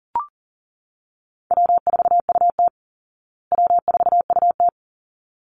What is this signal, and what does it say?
Morse code, heard as a keyed tone, about 40 wpm, 700 Hz. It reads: W4UT W4UT
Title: Call Signs Repeated 3x - 40wpm